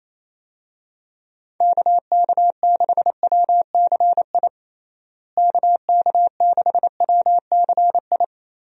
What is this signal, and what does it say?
Morse code, heard as a keyed tone, about 28 wpm, 700 Hz. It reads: KK6WCI KK6WCI
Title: USA Call Signs - 6 Characters - Repeated 2x - 28wpm